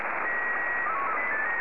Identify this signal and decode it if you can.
SSTV (slow-scan television), Scottie 1.